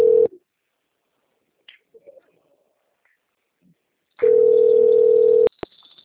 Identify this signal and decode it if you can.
Ringback tone (North America), 440 + 480 Hz.